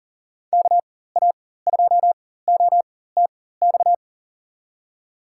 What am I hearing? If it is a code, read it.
Morse code, heard as a keyed tone, about 40 wpm, spaced as for 19 wpm, 700 Hz. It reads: KA2OTX
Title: USA Call Signs - 4x Character Spacing - 40wpm